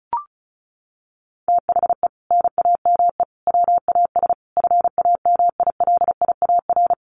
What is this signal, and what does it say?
Morse code, heard as a keyed tone, about 34 wpm, 700 Hz. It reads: THE NAME WAS FAMILIAR
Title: Sentences from Top 1000 Words 35wpm